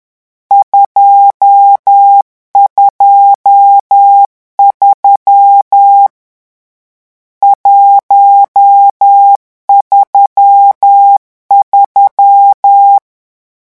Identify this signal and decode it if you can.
Morse code, heard as a keyed tone, about 10 wpm, 800 Hz. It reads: 223 133
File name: morse mp3 / lecon10-vitesse 10